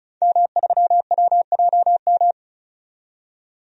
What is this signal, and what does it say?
Morse code, heard as a keyed tone, about 36 wpm, 700 Hz. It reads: M3WJM
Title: DXCC-100 Call Signs - 5 Characters - 35wpm